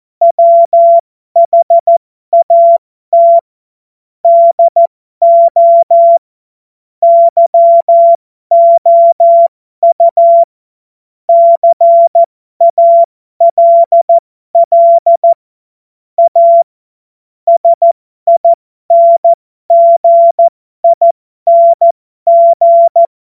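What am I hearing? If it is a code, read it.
Morse code, heard as a keyed tone, about 14 wpm, spaced as for 12 wpm, 680 Hz. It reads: WHAT DO YOU CALL A SINGING
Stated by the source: Morse code joke practice.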